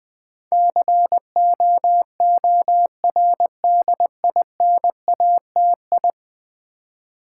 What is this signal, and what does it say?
Morse code, heard as a keyed tone, about 20 wpm, 700 Hz. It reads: COORDINATI
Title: Build A Word - 5 Letters At A Time - 12-Letter Words - 20wpm